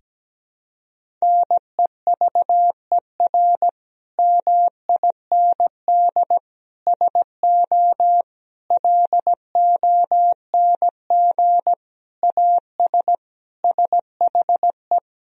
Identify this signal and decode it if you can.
Morse code, heard as a keyed tone, about 17 wpm, 700 Hz. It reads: NEVER MIND SO LONG AS SHE
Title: Sentences from Top 1000 Words - Review Part 1 - 17wpm